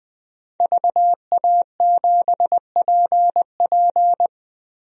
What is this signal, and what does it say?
Morse code, heard as a keyed tone, about 20 wpm, 700 Hz. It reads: VA7PP
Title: DXCC-100 Call Signs - 5 Characters - Repeated 3x - 20wpm